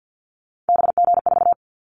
Morse code, heard as a keyed tone, about 50 wpm, 700 Hz. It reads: BG4
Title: Callsign Prefixes for Top 100 DXCC - 50wpm